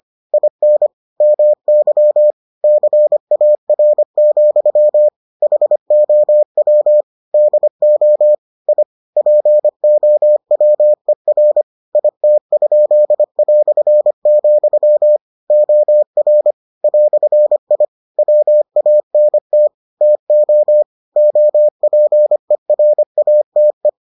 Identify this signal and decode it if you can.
Morse code, heard as a keyed tone, about 25 wpm, 600 Hz. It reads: IN MY CAR, HOW DO I POWER IT?", OR "I WANT TO OPERATE